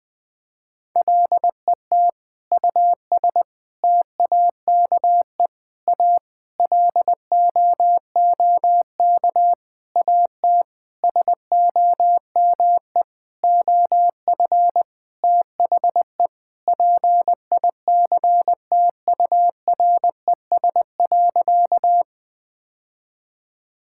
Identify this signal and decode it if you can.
Morse code, heard as a keyed tone, about 21 wpm, 700 Hz. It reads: LET US TAKE A LOOK AT SOME OF THE PICTURES.
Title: Sentences with 8-Word Phrases - 20wpm